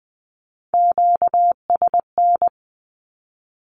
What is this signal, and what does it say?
Morse code, heard as a keyed tone, about 20 wpm, 700 Hz. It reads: QSN
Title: Q-Codes with Spoken Meaning - 20wpm